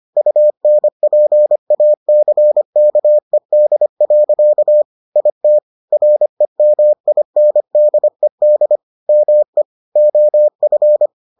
Morse code, heard as a keyed tone, about 25 wpm, 600 Hz. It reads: UNPACKED. IT REMINDED ME OF